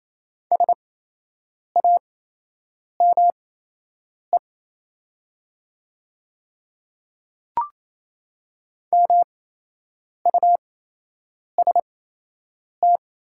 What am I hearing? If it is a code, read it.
Morse code, heard as a keyed tone, about 28 wpm, spaced as for 8 wpm, 700 Hz. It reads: SAME MUST
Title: 8x Character Spacing - Top 200 Words - 28wpm